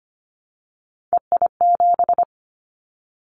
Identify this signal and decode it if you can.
Morse code, heard as a keyed tone, about 25 wpm, 700 Hz. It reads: EI7
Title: Callsign Prefixes for Top 50 DXCC - 25wpm